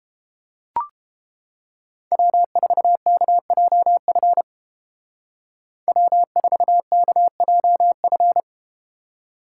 Morse code, heard as a keyed tone, about 32 wpm, 700 Hz. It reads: W4KJF W4KJF